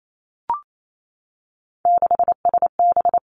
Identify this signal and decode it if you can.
Morse code, heard as a keyed tone, about 28 wpm, 700 Hz. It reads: 6SB
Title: Three Challenging Letter-Number Combinations 28wpm